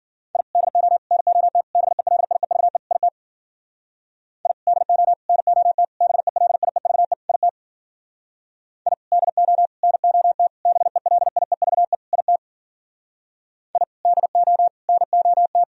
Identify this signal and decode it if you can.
Morse code, heard as a keyed tone, about 52 wpm, 700 Hz. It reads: I DO NOT BELIEVE IT I DO NOT BELIEVE IT I DO NOT BELIEVE IT I DO NOT